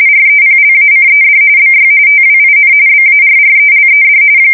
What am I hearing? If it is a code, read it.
RTTY, about 45.45 baud. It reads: REPLACED BY TWIN BARRELED
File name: fsk170 / battleshipnj